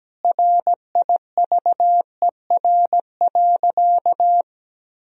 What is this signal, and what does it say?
Morse code, heard as a keyed tone, about 17 wpm, 700 Hz. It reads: RIVER.
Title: Sentences from Top 1000 Words - Review Part 4 - 17wpm